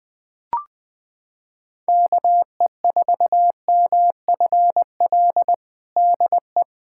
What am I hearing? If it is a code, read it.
Morse code, heard as a keyed tone, about 20 wpm, 700 Hz. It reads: KE4MFL DE
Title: USA Callsign DE Callsign 20wpm